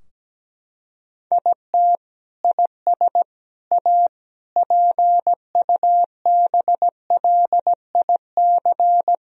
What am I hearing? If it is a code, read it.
Morse code, heard as a keyed tone, about 17 wpm, 700 Hz. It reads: IT IS A PUBLIC
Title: Sentences from Top 1000 Words - Review Part 3 - 17wpm